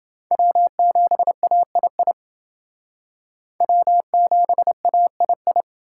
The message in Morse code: W7AII W7AII

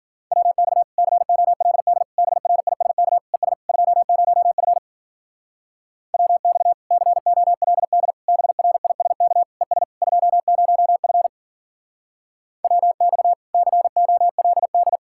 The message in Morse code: WX COLD BRISK ES 20F WX COLD BRISK ES 20F WX COLD

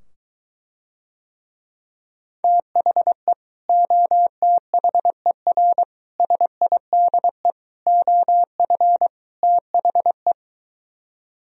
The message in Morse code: THE OTHER SIDE OF THE

